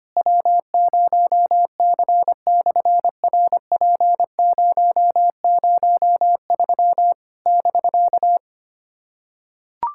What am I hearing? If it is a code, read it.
Morse code, heard as a keyed tone, about 24 wpm, 700 Hz. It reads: W0C/RP003 <BK>
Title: SOTA QSO Practice 25wpm